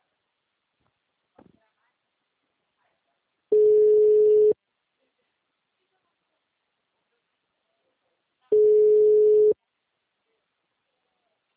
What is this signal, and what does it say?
Ringback tone (Europe, 425 Hz), with speech.
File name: out- -2016- - - .10